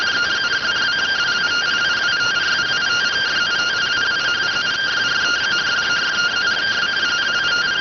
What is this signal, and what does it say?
Two-tone teleprinter signal, about 170 Hz shift.